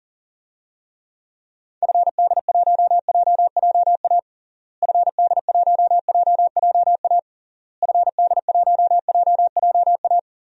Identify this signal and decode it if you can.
Morse code, heard as a keyed tone, about 37 wpm, 700 Hz. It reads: FD1JJA FD1JJA FD1JJA
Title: DXCC-100 Call Signs - 6 Characters - Repeated 3x - 40wpm